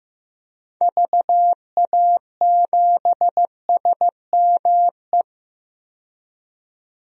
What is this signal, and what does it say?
Morse code, heard as a keyed tone, about 15 wpm, 700 Hz. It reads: VA7SME
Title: DXCC-100 Call Signs - 6 Characters - 15wpm